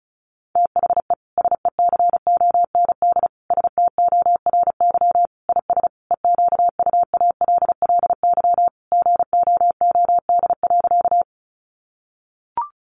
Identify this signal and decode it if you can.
Morse code, heard as a keyed tone, about 34 wpm, 700 Hz. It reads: THE SECOND STORY IS EQUALLY GOOD.
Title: Sentences from Top 1400 Words 35wpm